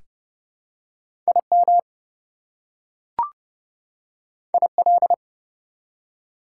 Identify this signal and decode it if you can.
Morse code, heard as a keyed tone, about 30 wpm, 700 Hz. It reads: IM IL
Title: Prefixes and Suffixes 30wpm